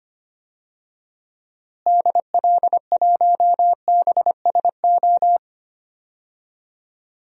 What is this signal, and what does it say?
Morse code, heard as a keyed tone, about 25 wpm, 700 Hz. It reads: DL1BSO
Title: DXCC-100 Call Signs - 6 Characters - Repeated 2x - 25wpm